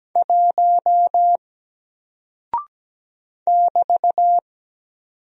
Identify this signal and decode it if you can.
Morse code, heard as a keyed tone, about 17 wpm, 700 Hz. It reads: 1 <BT>
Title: Single Character (Letter:Number:Punctuation:Prosign) Alt 17wpm